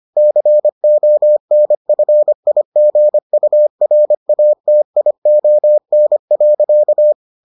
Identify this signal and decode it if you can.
Morse code, heard as a keyed tone, about 25 wpm, 600 Hz. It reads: CONFIGURATION.